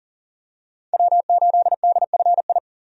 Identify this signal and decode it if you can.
Morse code, heard as a keyed tone, about 40 wpm, 700 Hz. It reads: W8DFI
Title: USA Call Signs - 5 Characters - 40wpm